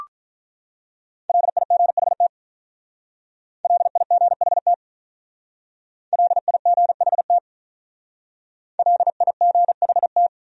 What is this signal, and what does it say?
Morse code, heard as a keyed tone, about 44 wpm, 700 Hz. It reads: LIGHT LIGHT LIGHT LIGHT